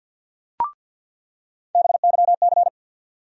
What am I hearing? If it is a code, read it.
Morse code, heard as a keyed tone, about 47 wpm, 700 Hz. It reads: BYC
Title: USA Call Sign Postfixes - 3 Letters 50wpm